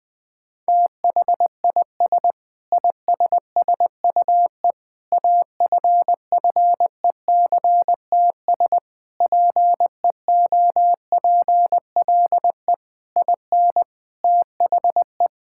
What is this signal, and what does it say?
Morse code, heard as a keyed tone, about 21 wpm, 700 Hz. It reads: THIS ISSUE AFFECTS PEOPLE IN THE
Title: Sentences with 7-Word Phrases - Spoken Beginning - 20wpm